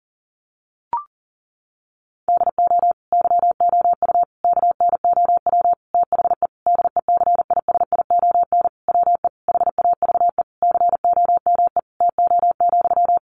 Morse code, heard as a keyed tone, about 37 wpm, 700 Hz. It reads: DO YOU KNOW THE DECISION WE HAVE COME TO,
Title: Sentences from Top 1000 Words 40wpm